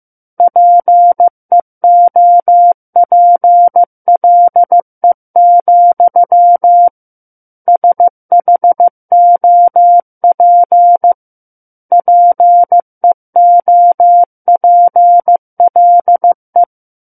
Morse code, heard as a keyed tone, about 15 wpm, 700 Hz. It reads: PEOPLE, SHOP PEOPLE
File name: Book One / 06-Chapter 6 - 15 WPM